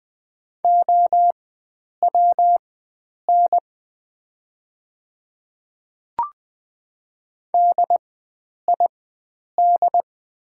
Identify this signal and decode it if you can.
Morse code, heard as a keyed tone, about 20 wpm, spaced as for 9 wpm, 700 Hz. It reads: OWN DID
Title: 4x Character Spacing - Top 300 Words - 20wpm